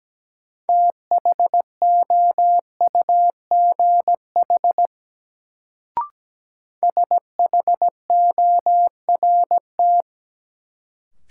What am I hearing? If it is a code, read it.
Morse code, heard as a keyed tone, about 17 wpm, 700 Hz. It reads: THOUGH SHORT
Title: Sentences from Top 400 Words - New Word Drill - 17wpm